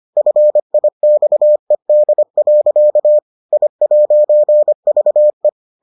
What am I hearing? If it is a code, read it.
Morse code, heard as a keyed tone, about 25 wpm, 600 Hz. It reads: FIXED. I'VE